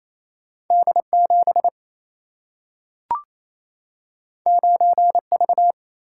Morse code, heard as a keyed tone, about 28 wpm, 700 Hz. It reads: D 7 9 V
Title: Single-Letter Single-Number 28wpm